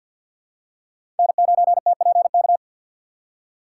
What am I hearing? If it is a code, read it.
Morse code, heard as a keyed tone, about 47 wpm, 700 Hz. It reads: N9TPK
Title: USA Call Signs - 2x Character Spacing - 50wpm